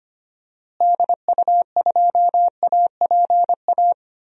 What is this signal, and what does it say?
Morse code, heard as a keyed tone, about 25 wpm, 700 Hz. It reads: DU2APA
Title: DXCC-100 Call Signs - 6 Characters - Repeated 3x - 25wpm